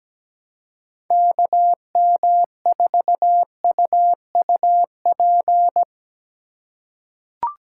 Morse code, heard as a keyed tone, about 17 wpm, 700 Hz. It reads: KM4UUP
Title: USA Call Signs - 4x Character Spacing - 17wpm